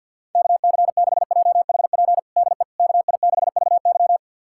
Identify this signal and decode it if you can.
Morse code, heard as a keyed tone, about 49 wpm, 700 Hz. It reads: KC6JHP DE KI6VY
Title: USA Callsign DE Callsign 50wpm